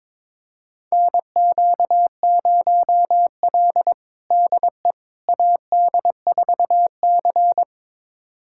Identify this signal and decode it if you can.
Morse code, heard as a keyed tone, about 22 wpm, 700 Hz. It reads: NQ0L DE AD4C
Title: USA Callsign DE Callsign - 4 Characters - 22wpm